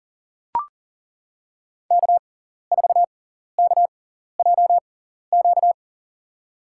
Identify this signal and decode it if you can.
Morse code, heard as a keyed tone, about 40 wpm, spaced as for 14 wpm, 700 Hz. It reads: K4KJQ
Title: USA Call Signs - 6x Character Spacing - 40wpm